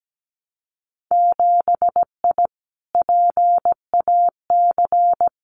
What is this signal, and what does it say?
Morse code, heard as a keyed tone, about 17 wpm, 700 Hz. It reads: 7I PAC
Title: Winter Field Day Exchanges Encore 17wpm